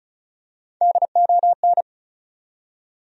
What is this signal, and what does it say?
Morse code, heard as a keyed tone, about 36 wpm, 700 Hz. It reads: DON
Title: USA Common Names - Max 4 Letters - 35wpm